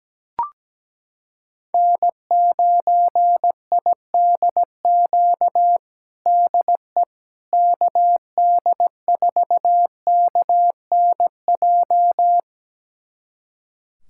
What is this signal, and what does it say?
Morse code, heard as a keyed tone, about 17 wpm, 700 Hz. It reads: N9IDQ DE KD4KNJ
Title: USA Callsign DE Callsign 17wpm